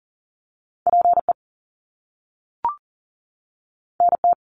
Morse code, heard as a keyed tone, about 40 wpm, 700 Hz. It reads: PE NT
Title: Canadian Provinces and Territories 40wpm